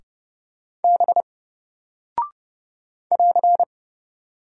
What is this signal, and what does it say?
Morse code, heard as a keyed tone, about 30 wpm, 700 Hz. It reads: B <AR>
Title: Single Character (Letter:Number:Punctuation:Prosign) 30wpm